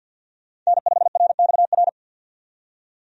Morse code, heard as a keyed tone, about 50 wpm, 700 Hz. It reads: N5RXR
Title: USA Call Signs - 5 Characters - 50wpm